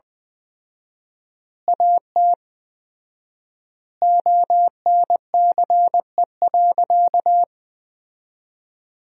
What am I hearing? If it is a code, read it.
Morse code, heard as a keyed tone, about 21 wpm, 700 Hz. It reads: AT ONCE.
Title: 4x Word-Spacing - Sentences from Top 500 Words - 20wpm